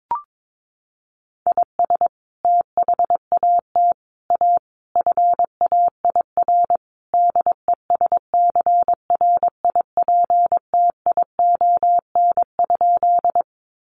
Morse code, heard as a keyed tone, about 22 wpm, 700 Hz. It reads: IS THAT A FAIR DESCRIPTION?
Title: Sentences from Top 1300 Words 22wpm